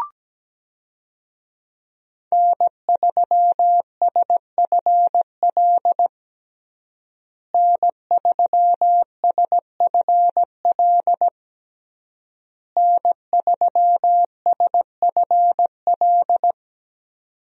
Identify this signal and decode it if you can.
Morse code, heard as a keyed tone, about 17 wpm, 700 Hz. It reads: N3SFL N3SFL N3SFL
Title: USA Call Signs - 5 Characters - Repeated 3x - 17wpm